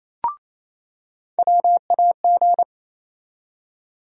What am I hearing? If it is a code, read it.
Morse code, heard as a keyed tone, about 28 wpm, 700 Hz. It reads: WAG
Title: USA Call Sign Postfixes - 3 Letters 28wpm